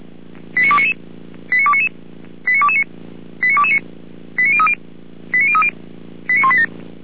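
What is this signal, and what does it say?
Five-tone selective call (ZVEI), heard as quick runs of five tones, tones of about 70 ms. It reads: 89299 89299 89229 89229 89030 89030 89187
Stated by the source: ZVEI five-tone selective call received on an SDR.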